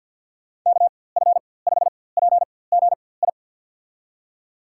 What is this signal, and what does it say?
Morse code, heard as a keyed tone, about 46 wpm, spaced as for 23 wpm, 700 Hz. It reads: KF5PGI
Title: USA Call Signs - 6 Characters - 4x Character Spacing - 50wpm